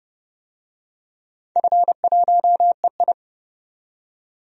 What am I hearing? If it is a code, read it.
Morse code, heard as a keyed tone, about 30 wpm, 700 Hz. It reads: F1EI
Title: DXCC-100 Call Signs - 4 Characters - 30wpm